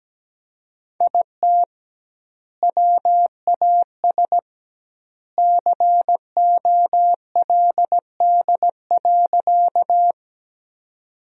Morse code, heard as a keyed tone, about 17 wpm, 700 Hz. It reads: IT WAS COLD.
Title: 2x Word-Spacing - Sentences from Top 400 Words - 17wpm